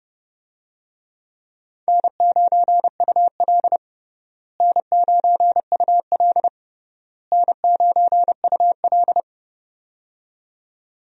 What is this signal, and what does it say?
Morse code, heard as a keyed tone, about 30 wpm, 700 Hz. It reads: N9UL N9UL N9UL
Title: USA Call Signs - 4 Characters - Repeated 3x - 30wpm